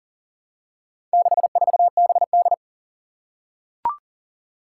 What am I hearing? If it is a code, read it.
Morse code, heard as a keyed tone, about 40 wpm, 700 Hz. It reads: 64BD
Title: Four Challenging Letter-Number Combinations 40wpm